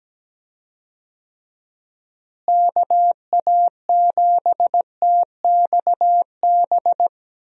Morse code, heard as a keyed tone, about 17 wpm, 700 Hz. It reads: KA7TXB
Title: USA Call Signs - 6 Characters - Repeated 2x - 17wpm